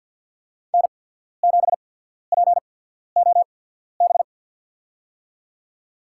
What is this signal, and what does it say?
Morse code, heard as a keyed tone, about 53 wpm, spaced as for 14 wpm, 700 Hz. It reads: N7POB
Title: USA Call Signs - 5 Characters - 8x Character Spacing - 50wpm